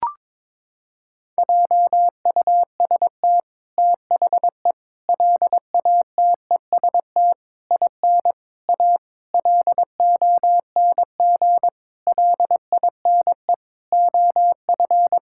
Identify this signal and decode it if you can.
Morse code, heard as a keyed tone, about 22 wpm, 700 Hz. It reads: JUST THE LATEST IN A LONG LINE OF
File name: practice / 22wpm / Sentences with 8-Word Phrases - 22wpm